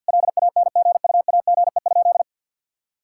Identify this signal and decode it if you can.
Morse code, heard as a keyed tone, about 50 wpm, 700 Hz. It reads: LANGUAGE?
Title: Sentences from Top 1000 Words - Review Part 5 - 50wpm